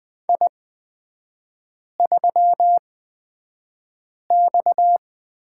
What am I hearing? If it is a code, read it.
Morse code, heard as a keyed tone, about 20 wpm, spaced as for 5 wpm, 700 Hz. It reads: I3X
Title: 255 Lesson 41e - Period - Three Characters 20wpm